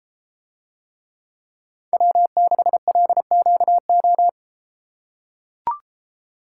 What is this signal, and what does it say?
Morse code, heard as a keyed tone, about 33 wpm, 700 Hz. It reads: W6LQO